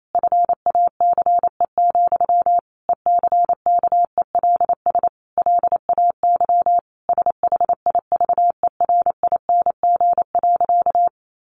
Morse code, heard as a keyed tone, about 28 wpm, 700 Hz. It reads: FACE, ECKELS LAY SHIVERING.
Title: A Sound Of Thunder 28wpm